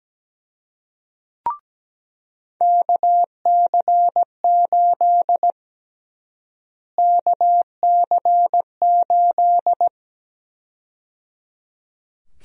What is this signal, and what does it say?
Morse code, heard as a keyed tone, about 17 wpm, 700 Hz. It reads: KC8 KC8
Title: USA Call Sign Prefixes - Repeated 2x - 17wpm